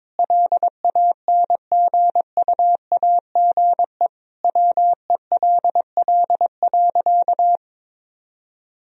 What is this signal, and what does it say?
Morse code, heard as a keyed tone, about 22 wpm, 700 Hz. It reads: LANGUAGE WELL.
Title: Sentences from Top 1000 Words - Review Part 3 - 22wpm